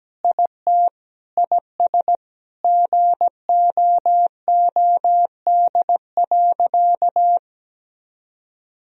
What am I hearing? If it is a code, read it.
Morse code, heard as a keyed tone, about 17 wpm, 700 Hz. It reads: IT IS GOOD.
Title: Sentences from Top 1000 Words - Review Part 3 - 17wpm